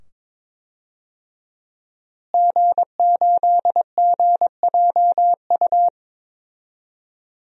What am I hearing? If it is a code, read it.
Morse code, heard as a keyed tone, about 22 wpm, 700 Hz. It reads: G8GJU
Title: DXCC-100 Call Signs - 5 Characters - Repeated 2x - 22wpm